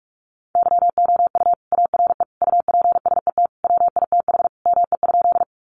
Morse code, heard as a keyed tone, about 44 wpm, 700 Hz. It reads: YOU ARE UPSET WITH ME?
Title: Sentences from Top 800 Words 45wpm